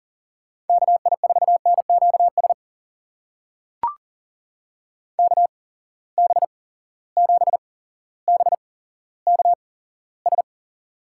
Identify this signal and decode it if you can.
Morse code, heard as a keyed tone, about 38 wpm, spaced as for 17 wpm, 700 Hz. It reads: KI4NQS KB7BKS
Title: USA Call Signs - 6 Characters - 8x Character Spacing - 40wpm